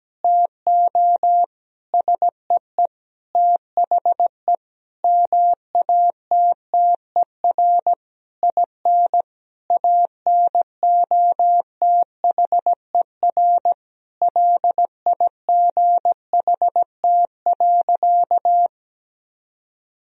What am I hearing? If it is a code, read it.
Morse code, heard as a keyed tone, about 17 wpm, 700 Hz. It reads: TO SEE THE MATTER IN ANOTHER LIGHT.
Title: Sentences from Top 1000 Words - Review Part 1 - 17wpm